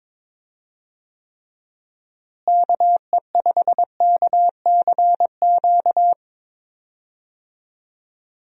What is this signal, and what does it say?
Morse code, heard as a keyed tone, about 23 wpm, 700 Hz. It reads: KE5KCQ
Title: USA Call Signs - 6 Characters - Repeated 2x - 22wpm